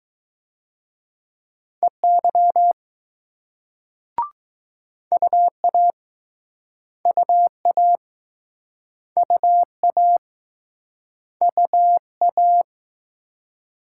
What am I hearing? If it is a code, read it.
Morse code, heard as a keyed tone, about 21 wpm, 700 Hz. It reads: EY UA UA UA UA